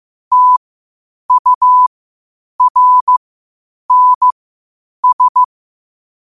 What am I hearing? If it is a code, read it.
Morse code, heard as a keyed tone, about 15 wpm, spaced as for 8 wpm, 1000 Hz. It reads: TURNS
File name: qotdfiles / 8wpm